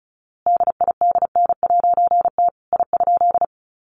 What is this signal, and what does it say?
Morse code, heard as a keyed tone, about 34 wpm, 700 Hz. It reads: DIDN'T I?
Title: A Sound Of Thunder 35wpm